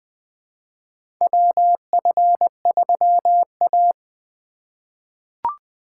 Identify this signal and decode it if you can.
Morse code, heard as a keyed tone, about 20 wpm, 700 Hz. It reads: WF3A
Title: USA Call Signs - 3x Character Spacing - 20wpm